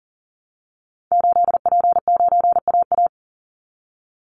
Morse code, heard as a keyed tone, about 39 wpm, 700 Hz. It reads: 8P9AA
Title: Call Signs Repeated 3x - 40wpm